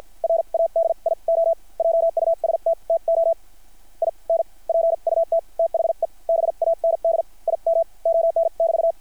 Morse code, heard as a keyed tone, about 50 wpm, spaced as for 40 wpm, 660 Hz. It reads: RADIO JUST TO INPUT THE BAND IM ON<BT>
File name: archived / CRSnet